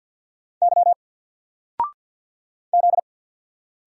Morse code, heard as a keyed tone, about 48 wpm, 700 Hz. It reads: Y Z
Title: Single Character (Letter:Number:Punctuation:Prosign) 50wpm